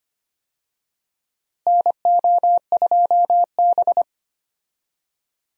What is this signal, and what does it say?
Morse code, heard as a keyed tone, about 25 wpm, 700 Hz. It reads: NO2B
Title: USA Call Signs - 4 Characters - Repeated 3x - 25wpm